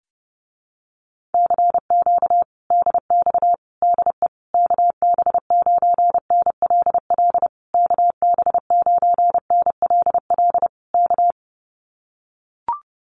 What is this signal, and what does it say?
Morse code, heard as a keyed tone, about 30 wpm, 700 Hz. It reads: CQ DX DE KB9NLL KB9NLL K
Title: Common QSO Phrases Encore 30wpm